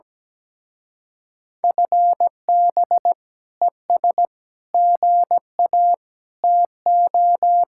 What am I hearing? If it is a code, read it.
Morse code, heard as a keyed tone, about 20 wpm, 700 Hz. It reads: FB ES GA TO
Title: Speed-Racing Common QSO Phrases Encore - 17wpm